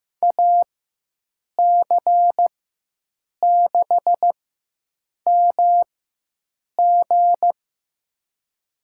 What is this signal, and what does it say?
Morse code, heard as a keyed tone, about 15 wpm, spaced as for 7 wpm, 700 Hz. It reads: AC6MG
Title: USA Call Signs - 4x Character Spacing - 15wpm